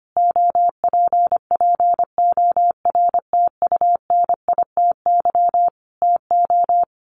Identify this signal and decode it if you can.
Morse code, heard as a keyed tone, about 25 wpm, 700 Hz. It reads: OPPORTUNITY TO
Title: Sentences from Top 1000 Words 25wpm